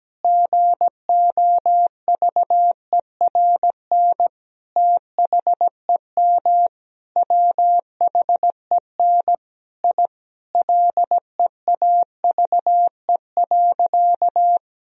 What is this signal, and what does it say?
Morse code, heard as a keyed tone, about 17 wpm, 700 Hz. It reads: GOVERN THEM WHEN I LEAVE.
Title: Sentences from Top 1000 Words - Review Part 2 - 17wpm